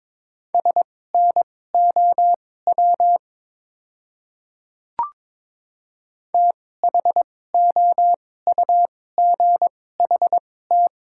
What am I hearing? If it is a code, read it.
Morse code, heard as a keyed tone, about 21 wpm, spaced as for 16 wpm, 690 Hz. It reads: SNOW THOUGHT
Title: 2x Character Spacing - Top 500 Words - 22wpm